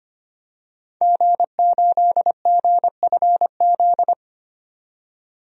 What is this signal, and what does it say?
Morse code, heard as a keyed tone, about 25 wpm, 700 Hz. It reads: G8GFZ
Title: DXCC-100 Call Signs - 5 Characters - 25wpm